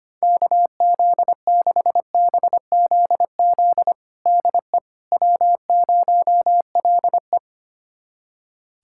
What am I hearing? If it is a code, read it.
Morse code, heard as a keyed tone, about 25 wpm, 700 Hz. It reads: KZ6BZZ DE W0LE
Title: USA Callsign DE Callsign 25wpm